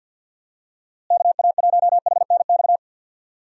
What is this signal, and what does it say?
Morse code, heard as a keyed tone, about 47 wpm, 700 Hz. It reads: KA1HNX
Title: USA Call Signs - 6 Characters - Repeated 2x - 50wpm